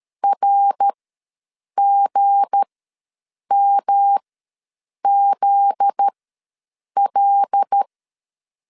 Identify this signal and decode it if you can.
Morse code, heard as a keyed tone, about 13 wpm, spaced as for 7 wpm, 800 Hz. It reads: RGMZL
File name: 7 GF9Z ws 12wpm